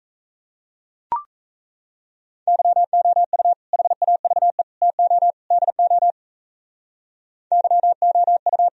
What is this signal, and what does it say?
Morse code, heard as a keyed tone, about 35 wpm, 700 Hz. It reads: YOU HAVE TO DO YOU
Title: Speed-Racing - Sets of 4 Words 28wpm